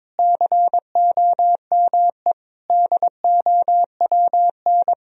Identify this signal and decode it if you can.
Morse code, heard as a keyed tone, about 22 wpm, 700 Hz. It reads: COME DOWN
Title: Sentences from Top 1000 Words - Review Part 2 - 22wpm